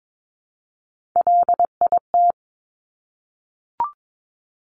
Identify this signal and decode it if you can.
Morse code, heard as a keyed tone, about 22 wpm, 700 Hz. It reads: LIT
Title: Top Three Letter Combinations in Words 22wpm